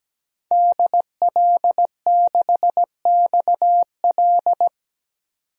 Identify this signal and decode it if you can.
Morse code, heard as a keyed tone, about 17 wpm, 700 Hz. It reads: DL6XL
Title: DXCC-100 Call Signs - 5 Characters - 17wpm